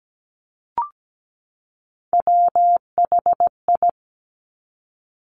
Morse code, heard as a keyed tone, about 17 wpm, 700 Hz. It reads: WHI
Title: Top Three Letter Combinations in Words 17wpm